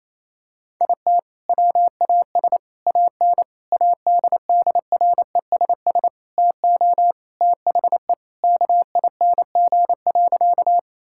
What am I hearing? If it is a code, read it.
Morse code, heard as a keyed tone, about 27 wpm, 700 Hz. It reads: IT WAS AN ADDRESS TO THE KING.
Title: Sentences from Top 1000 Words - Review Part 6 - 28wpm